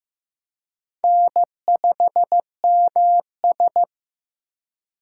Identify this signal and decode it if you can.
Morse code, heard as a keyed tone, about 18 wpm, 700 Hz. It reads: N5MS
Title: Speed Racing - USA 1x2 Call Signs 15wpm